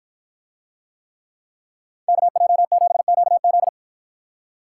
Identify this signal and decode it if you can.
Morse code, heard as a keyed tone, about 53 wpm, 700 Hz. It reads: KJ7QZ